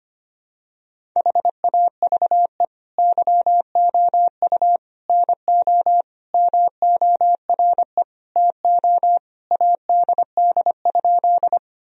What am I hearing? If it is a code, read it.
Morse code, heard as a keyed tone, about 25 wpm, 700 Hz. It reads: HAVE YOU NO MORE TO ADD?
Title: Sentences from Top 1000 Words - Review Part 1 - 25wpm